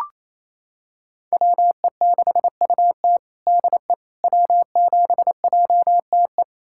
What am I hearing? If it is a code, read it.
Morse code, heard as a keyed tone, about 28 wpm, 700 Hz. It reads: WE6UT DE W7JTE
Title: USA Callsign DE Callsign - 5 Characters - 28wpm